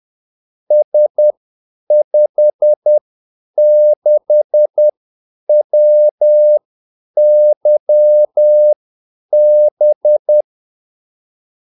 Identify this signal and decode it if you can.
Morse code, heard as a keyed tone, about 10 wpm, spaced as for 8 wpm, 600 Hz. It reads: S56WYB
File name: Callsigns-10-8wpm